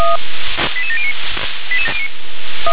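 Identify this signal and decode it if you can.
Five-tone selective call (ZVEI), heard as quick runs of five tones, tones of about 70 ms. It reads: 88709
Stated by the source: ZVEI five-tone selective call received on an SDR.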